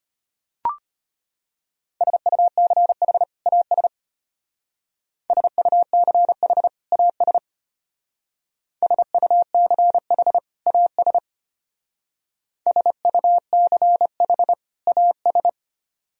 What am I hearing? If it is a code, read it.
Morse code, heard as a keyed tone, about 30 wpm, 700 Hz. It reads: SUCH AS SUCH AS SUCH AS SUCH AS
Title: Speed-Racing - Sets of 2 Words 25wpm